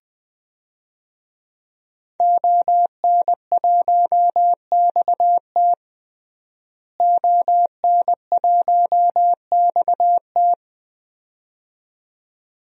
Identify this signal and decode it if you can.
Morse code, heard as a keyed tone, about 20 wpm, 700 Hz. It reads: ON1XT ON1XT
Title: DXCC-100 Call Signs - 5 Characters - Repeated 2x - 20wpm